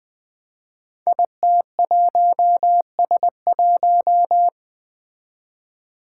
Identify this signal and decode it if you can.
Morse code, heard as a keyed tone, about 20 wpm, 700 Hz. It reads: IT1S1
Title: Five Challenging Letter-Number Combinations 20wpm